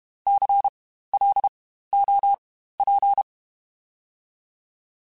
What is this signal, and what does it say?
Morse code, heard as a keyed tone, about 32 wpm, spaced as for 15 wpm, 800 Hz. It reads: CLOP
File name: Spaced Alphas / Spaced Alphas mp3s / Cs sp @30